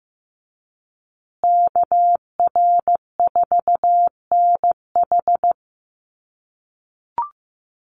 Morse code, heard as a keyed tone, about 15 wpm, 700 Hz. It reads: KR4NH
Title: USA Call Signs Repeated 3x 15wpm